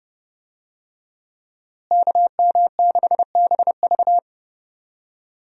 Morse code, heard as a keyed tone, about 31 wpm, 700 Hz. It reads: KM6BV
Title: USA Call Signs - 5 Characters - Repeated 2x - 30wpm